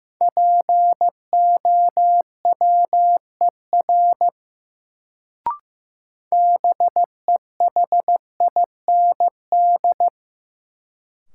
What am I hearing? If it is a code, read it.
Morse code, heard as a keyed tone, about 15 wpm, 700 Hz. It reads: POWER BEHIND
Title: Sentences from Top 500 Words - New Word Drill - 15wpm